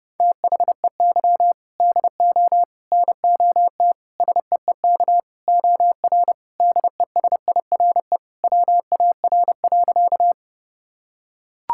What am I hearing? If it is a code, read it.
Morse code, heard as a keyed tone, about 30 wpm, 700 Hz. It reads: THEY DO NOT SEEK OR DESIRE WAR.
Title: Sentences from Top 1600 Words 30wpm